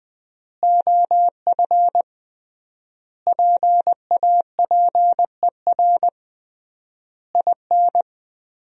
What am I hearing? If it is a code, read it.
Morse code, heard as a keyed tone, about 21 wpm, 700 Hz. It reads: OF PAPER IN
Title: 3x Word-Spacing - Sentences from Top 400 Words - 20wpm